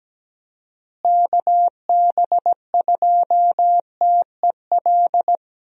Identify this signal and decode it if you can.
Morse code, heard as a keyed tone, about 17 wpm, 700 Hz. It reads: KB2TEL
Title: USA Call Signs - 6 Characters - Repeated 3x - 17wpm